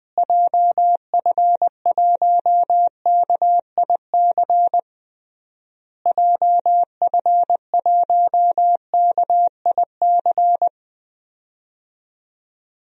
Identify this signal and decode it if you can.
Morse code, heard as a keyed tone, about 20 wpm, 700 Hz. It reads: JF1KIC JF1KIC
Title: DXCC-100 Call Signs - 6 Characters - Repeated 3x - 20wpm